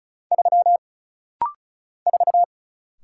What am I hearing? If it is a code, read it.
Morse code, heard as a keyed tone, about 36 wpm, 700 Hz. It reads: 3 4
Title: Single Letter-Number - Rapid-Fire 35wpm